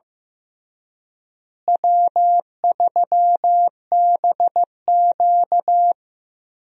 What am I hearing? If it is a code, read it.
Morse code, heard as a keyed tone, about 15 wpm, 700 Hz. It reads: W3BQ
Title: USA Call Signs - 4 Characters - Repeated 2x - 15wpm